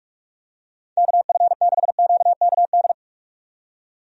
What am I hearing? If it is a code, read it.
Morse code, heard as a keyed tone, about 44 wpm, 700 Hz. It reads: KF6QKD
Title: USA Call Signs - 6 Characters - 45wpm